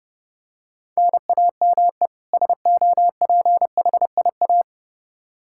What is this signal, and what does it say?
Morse code, heard as a keyed tone, about 30 wpm, 700 Hz. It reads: NAME SOPHIA
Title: Practice QSO - Exchange 1 - Beginning - 30wpm